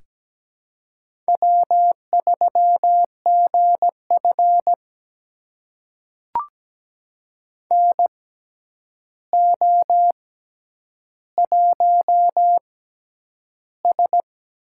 Morse code, heard as a keyed tone, about 17 wpm, spaced as for 9 wpm, 700 Hz. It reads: W3GF NO1S
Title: USA Call Signs - 6x Character Spacing - 17wpm